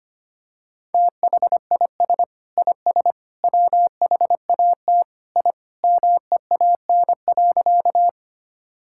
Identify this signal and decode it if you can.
Morse code, heard as a keyed tone, about 25 wpm, 700 Hz. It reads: THIS IS WHAT I MEAN.
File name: practice / 25wpm / Sentences with 3-Word Phrases - 25wpm